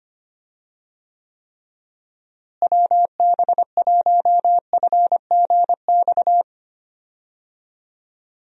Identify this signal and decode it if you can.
Morse code, heard as a keyed tone, about 25 wpm, 700 Hz. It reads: WB1FGX
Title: USA Call Signs - 6 Characters - Repeated 3x - 25wpm